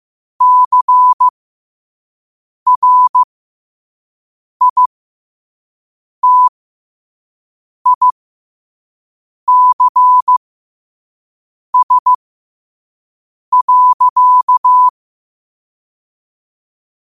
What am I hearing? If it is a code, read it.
Morse code, heard as a keyed tone, about 15 wpm, spaced as for 5 wpm, 1000 Hz. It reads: CRITICS.